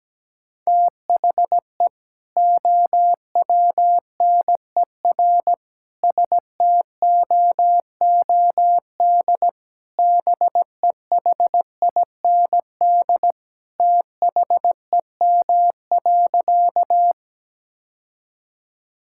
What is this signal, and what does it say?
Morse code, heard as a keyed tone, about 17 wpm, 700 Hz. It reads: THE OWNER STOOD BEHIND THEM.
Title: Sentences from Top 1600 Words 17wpm